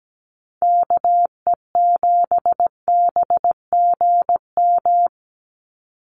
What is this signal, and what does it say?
Morse code, heard as a keyed tone, about 17 wpm, 700 Hz. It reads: KE7BGM
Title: NA SOTA Activator Callsigns 17wpm